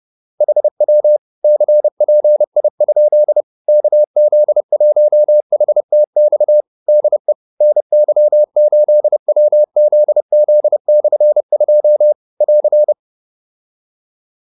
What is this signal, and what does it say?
Morse code, heard as a keyed tone, about 30 wpm, 600 Hz. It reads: HW CPI? KZ1HTX DE NY8WZZ/2 <AR>
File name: QSO301 30